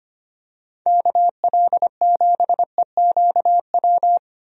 Morse code, heard as a keyed tone, about 25 wpm, 700 Hz. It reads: KL7EQW
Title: DXCC-100 Call Signs - 6 Characters - Repeated 3x - 25wpm